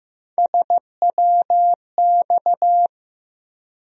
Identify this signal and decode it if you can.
Morse code, heard as a keyed tone, about 15 wpm, 700 Hz. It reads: SWX
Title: USA Call Sign Postfixes - 3 Letters 15wpm